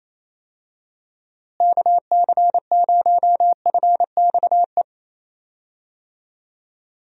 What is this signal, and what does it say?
Morse code, heard as a keyed tone, about 27 wpm, 700 Hz. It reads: KC0FXE
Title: USA Call Signs - 6 Characters - Repeated 2x - 28wpm